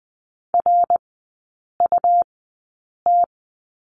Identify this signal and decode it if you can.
Morse code, heard as a keyed tone, about 20 wpm, spaced as for 8 wpm, 700 Hz. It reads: RUT
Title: 191 Lesson 31e - Letter v - Three Characters 20wpm